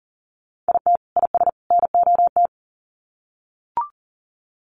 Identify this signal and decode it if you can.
Morse code, heard as a keyed tone, about 40 wpm, 700 Hz. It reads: IT IS NOT
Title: Sets of 3 Words 40wpm